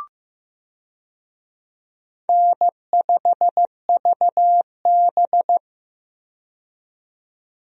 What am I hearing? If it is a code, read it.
Morse code, heard as a keyed tone, about 15 wpm, 700 Hz. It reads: N5VB
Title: USA Call Signs - 4 Characters - 15wpm